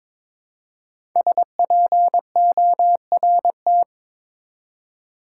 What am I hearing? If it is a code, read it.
Morse code, heard as a keyed tone, about 22 wpm, 700 Hz. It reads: SPORT
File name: 5 Letter Words - First 3 Letters Given - 22wpm